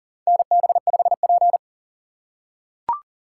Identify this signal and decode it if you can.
Morse code, heard as a keyed tone, about 40 wpm, 700 Hz. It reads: NB5P
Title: USA Call Signs - 4 Characters - 3x Character Spacing - 40wpm